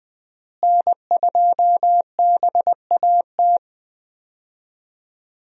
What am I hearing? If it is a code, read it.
Morse code, heard as a keyed tone, about 21 wpm, 700 Hz. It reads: N2BAT
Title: USA Call Signs - 5 Characters - Repeated 2x - 20wpm